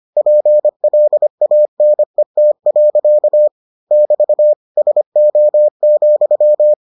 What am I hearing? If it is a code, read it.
Morse code, heard as a keyed tone, about 25 wpm, 600 Hz. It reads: PLANET. <BT> SO,